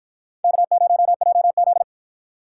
Morse code, heard as a keyed tone, about 48 wpm, 700 Hz. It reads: K0JZ